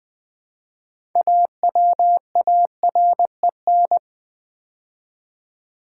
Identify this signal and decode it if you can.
Morse code, heard as a keyed tone, about 20 wpm, 700 Hz. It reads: AWAREN
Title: Build A Word - 3 Letters At A Time - 9-Letter Words - 20wpm